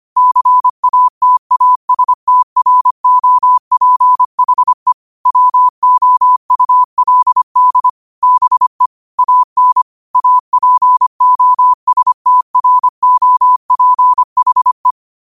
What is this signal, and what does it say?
Morse code, heard as a keyed tone, about 25 wpm, 1000 Hz. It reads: CATASTROPHE WOULD BE AN APOSTROPHE
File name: qotdfiles / 25wpm